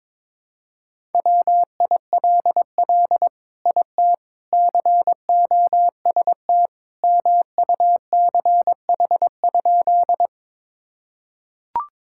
Morse code, heard as a keyed tone, about 22 wpm, 700 Hz. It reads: WILL IT COST MUCH?
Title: Sentences from Top 1000 Words - Review Part 5 - 22wpm